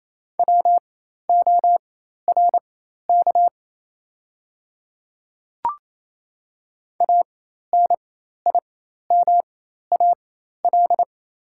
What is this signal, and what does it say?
Morse code, heard as a keyed tone, about 28 wpm, spaced as for 13 wpm, 700 Hz. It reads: WORK ANIMAL
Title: 4x Character Spacing - Top 300 Words - 28wpm